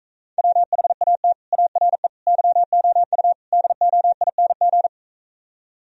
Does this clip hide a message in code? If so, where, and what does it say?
Yes: Morse code from 0.4 to 4.9 s, heard as a keyed tone, about 44 wpm, 700 Hz. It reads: WHAT ARE YOU DOING